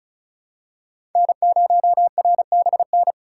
Morse code, heard as a keyed tone, about 36 wpm, 700 Hz. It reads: N0RBN
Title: USA Call Signs - 5 Characters - 4x Character Spacing - 35wpm